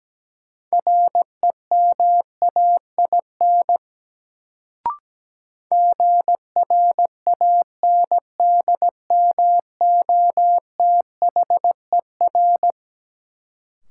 Morse code, heard as a keyed tone, about 17 wpm, 700 Hz. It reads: REMAIN GRANDMOTHER
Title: Sentences from Top 1200 Words - New Word Drill Alt - 17wpm